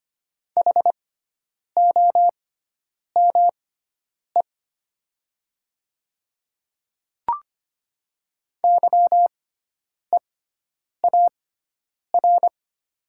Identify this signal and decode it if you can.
Morse code, heard as a keyed tone, about 25 wpm, spaced as for 9 wpm, 700 Hz. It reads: HOME YEAR
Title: 6x Character Spacing - Top 200 Words - 25wpm